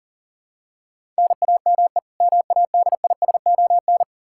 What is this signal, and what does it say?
Morse code, heard as a keyed tone, about 40 wpm, 700 Hz. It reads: NAME MADISON
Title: Practice QSO - Exchange 1 - Beginning - 40wpm